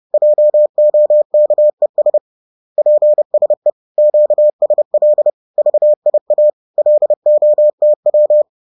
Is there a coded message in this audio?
Yes: Morse code from 0.1 to 8.4 s, heard as a keyed tone, about 30 wpm, 600 Hz. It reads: JOKES PSE QSL VIA LOTW